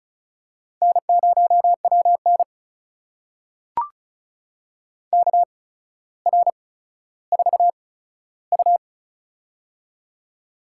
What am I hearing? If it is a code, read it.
Morse code, heard as a keyed tone, about 36 wpm, spaced as for 15 wpm, 700 Hz. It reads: N0WN KR4U
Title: USA Call Signs - 4 Characters - 8x Character Spacing - 35wpm